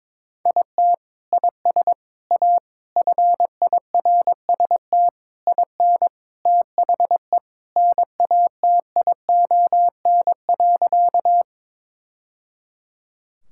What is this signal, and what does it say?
Morse code, heard as a keyed tone, about 22 wpm, 700 Hz. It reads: IT IS A FIRST IN THE NATION.
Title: Sentences from Top 1000 Words - Review Part 6 - 22wpm